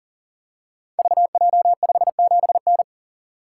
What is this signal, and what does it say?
Morse code, heard as a keyed tone, about 39 wpm, 700 Hz. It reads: VJ57N
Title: Code Groups - 5 Letter:Numbers - 40wpm